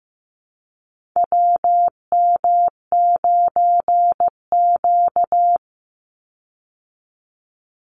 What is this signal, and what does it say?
Morse code, heard as a keyed tone, about 15 wpm, 700 Hz. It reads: WM9Q
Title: Call Signs Repeated 3x - 15wpm